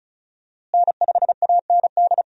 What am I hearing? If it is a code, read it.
Morse code, heard as a keyed tone, about 34 wpm, 700 Hz. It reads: N5AND
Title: USA Call Signs - 5 Characters - Repeated 2x - 35wpm